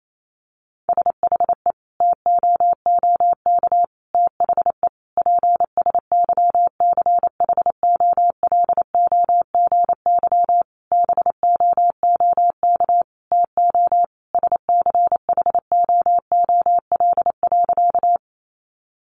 Morse code, heard as a keyed tone, about 28 wpm, 700 Hz. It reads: SHE TOOK THE PSYCHOLOGY BOOK TO SCHOOL.
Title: Sentences from Top 1400 Words 28wpm